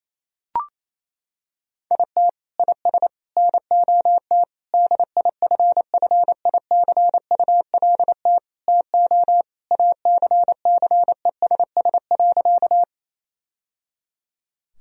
Morse code, heard as a keyed tone, about 28 wpm, 700 Hz. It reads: IT IS NOT DIFFICULT TO ACCESS.
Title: Sentences from Top 1000 Words - Review Part 4 - 28wpm